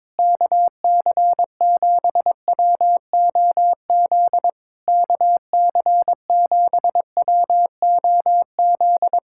Morse code, heard as a keyed tone, about 22 wpm, 700 Hz. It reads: KC7WOZ KC7WOZ
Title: Speed-Racing Common QSO Phrases - 22wpm